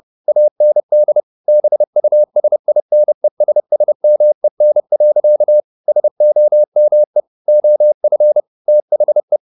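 Morse code, heard as a keyed tone, about 30 wpm, 600 Hz. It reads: AND BUSINESSMEN. SOME OF THE
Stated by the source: Morse code at 30 wpm, the novel Never by Ken Follett in English, from CW med Gnister.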